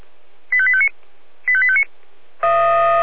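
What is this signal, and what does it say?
Five-tone selective call (ZVEI), heard as quick runs of five tones, tones of about 70 ms. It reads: 85859 85859
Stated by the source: ZVEI five-tone selective call 85820, received by SDR.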